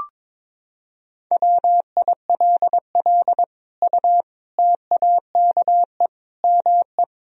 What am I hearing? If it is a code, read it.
Morse code, heard as a keyed tone, about 22 wpm, 700 Hz. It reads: WILL U TAKE ME
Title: Sentences from Top 200 Words and Abbreviations 22wpm